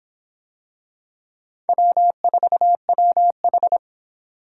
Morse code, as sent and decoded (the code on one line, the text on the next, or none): .-- ....- .-- ....
W4WH